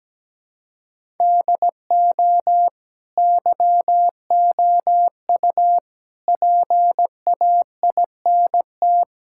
Morse code, heard as a keyed tone, about 17 wpm, 700 Hz. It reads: DO YOU PAINT
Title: Sentences from Top 1000 Words - Review Part 6 - 17wpm